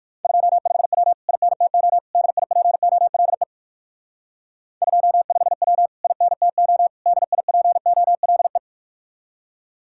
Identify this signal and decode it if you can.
Morse code, heard as a keyed tone, about 56 wpm, 700 Hz. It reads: 25W INTO DIPOLE 25W INTO DIPOLE
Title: Speed-Racing Common QSO Phrases - 45wpm